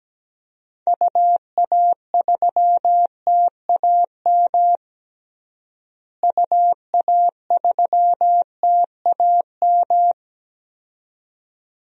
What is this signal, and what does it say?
Morse code, heard as a keyed tone, about 17 wpm, 700 Hz. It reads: UA3TAM UA3TAM
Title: DXCC-100 Call Signs - 6 Characters - Repeated 2x - 17wpm